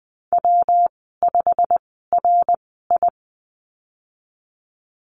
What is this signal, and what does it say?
Morse code, heard as a keyed tone, about 20 wpm, spaced as for 15 wpm, 700 Hz. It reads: W5RI
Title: 167 Lesson 27g - Number 7 - Call Signs 20wpm